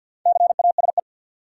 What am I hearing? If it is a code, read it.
Morse code, heard as a keyed tone, about 51 wpm, 700 Hz. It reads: CASE
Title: Sentences from Top 700 Words - New Word Drill - 50wpm